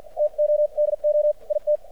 Morse code, heard as a keyed tone, about 53 wpm, spaced as for 43 wpm, 600 Hz, with radio noise. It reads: TODO IT
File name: archived / CRSnet